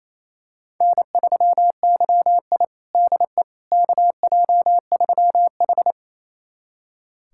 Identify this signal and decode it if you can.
Morse code, heard as a keyed tone, about 28 wpm, 700 Hz. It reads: N3YI DE KJ3H
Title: USA Callsign DE Callsign - 4 Characters - 28wpm